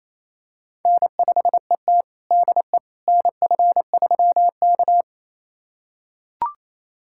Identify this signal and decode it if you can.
Morse code, heard as a keyed tone, about 28 wpm, 700 Hz. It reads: N5ET DE NF3K
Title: USA Callsign DE Callsign - 4 Characters - 28wpm